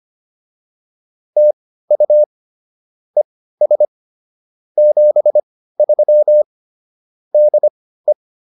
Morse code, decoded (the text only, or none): TU ES 73 DE